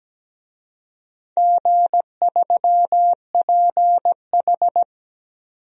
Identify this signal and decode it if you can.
Morse code, heard as a keyed tone, about 17 wpm, 700 Hz. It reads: G3PH
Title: DXCC-100 Call Signs - 4 Characters - 17wpm